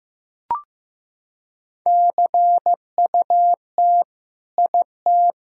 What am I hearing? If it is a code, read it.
Morse code, heard as a keyed tone, about 15 wpm, 700 Hz. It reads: CUT IT
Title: Sentences from Top 1000 Words - Review Part 1 - 15wpm